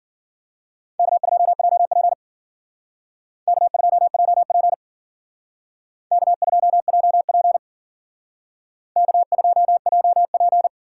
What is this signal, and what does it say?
Morse code, heard as a keyed tone, about 51 wpm, 700 Hz. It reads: K2JP K2JP K2JP K2JP